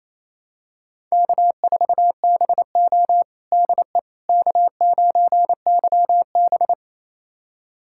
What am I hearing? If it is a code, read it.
Morse code, heard as a keyed tone, about 28 wpm, 700 Hz. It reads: K4BO DE K9YB
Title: USA Callsign DE Callsign - 4 Characters - 28wpm